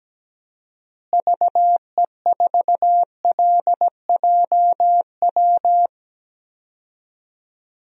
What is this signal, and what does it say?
Morse code, heard as a keyed tone, about 17 wpm, 700 Hz. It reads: VE4LJW
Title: DXCC-100 Call Signs - 6 Characters - Repeated 2x - 17wpm